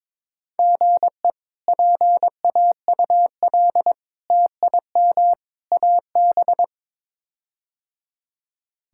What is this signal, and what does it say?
Morse code, heard as a keyed tone, about 22 wpm, 700 Hz. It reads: GE PAUL TIM AB
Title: K1USN SST QSO Practice - 22wpm